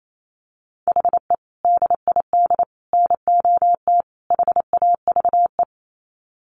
Morse code, heard as a keyed tone, about 28 wpm, 700 Hz. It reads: HE DID NOT HAVE
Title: Sets of 4 Words 28wpm